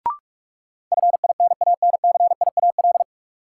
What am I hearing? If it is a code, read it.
Morse code, heard as a keyed tone, about 41 wpm, 700 Hz. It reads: FINANCIAL
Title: Sentences from Top 1000 Words - Review Part 5 - New Word Drill - 45wpm